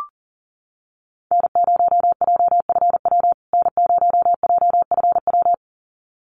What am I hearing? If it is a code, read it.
Morse code, heard as a keyed tone, about 40 wpm, 700 Hz. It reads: N0JFW N0JFW
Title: USA Call Signs Repeated 2x 40wpm